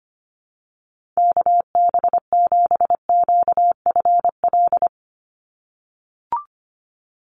Morse code, heard as a keyed tone, about 25 wpm, 700 Hz. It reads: KB7QFL
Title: USA Call Signs 25wpm